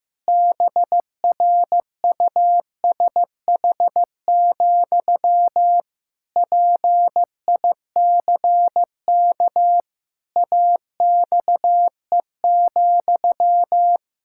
Morse code, heard as a keyed tone, about 15 wpm, 700 Hz. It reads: BRUSH, PICK AXE,
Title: Pangram Challenge v2 - 15wpm